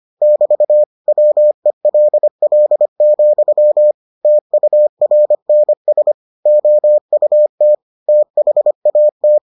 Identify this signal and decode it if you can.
Morse code, heard as a keyed tone, about 25 wpm, 600 Hz. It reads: <BT> WELL, TURNS OUT THAT